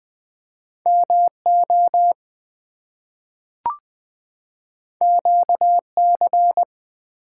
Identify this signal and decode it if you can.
Morse code, heard as a keyed tone, about 20 wpm, 700 Hz. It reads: MO QC
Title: US States and Canadian Provinces 20wpm